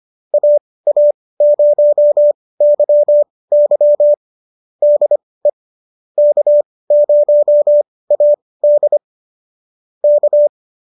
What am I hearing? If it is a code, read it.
Morse code, heard as a keyed tone, about 25 wpm, spaced as for 18 wpm, 590 Hz. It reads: AA0YY DE K0AD K